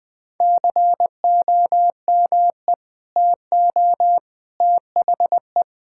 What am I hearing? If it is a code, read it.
Morse code, heard as a keyed tone, about 20 wpm, 700 Hz. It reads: COME TO THE
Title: Sentences with 2-Word Phrases - 20wpm